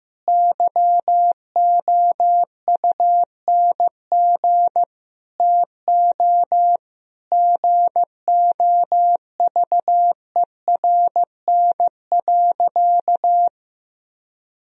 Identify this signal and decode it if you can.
Morse code, heard as a keyed tone, about 15 wpm, 700 Hz. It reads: YOUNG TO GOVERN.
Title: Sentences from Top 1000 Words - Review Part 2 - 15wpm